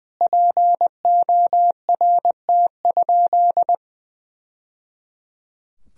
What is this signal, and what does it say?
Morse code, heard as a keyed tone, about 20 wpm, 700 Hz. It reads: PORT?
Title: Sentences from Top 1000 Words - Review Part 1 - 20wpm